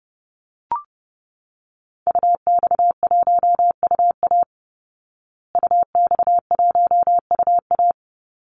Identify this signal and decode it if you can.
Morse code, heard as a keyed tone, about 30 wpm, 700 Hz. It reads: UX1UA UX1UA
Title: Call Signs Repeated 3x - 30wpm